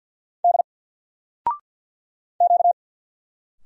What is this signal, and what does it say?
Morse code, heard as a keyed tone, about 48 wpm, 700 Hz. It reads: D Q
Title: Single Letter-Number Alt 50wpm